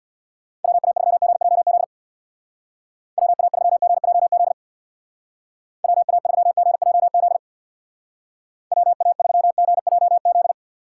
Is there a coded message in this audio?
Yes: Morse code from 0.6 to 10.5 s, heard as a keyed tone, about 63 wpm, 700 Hz. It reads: WA3GJZ WA3GJZ WA3GJZ WA3GJZ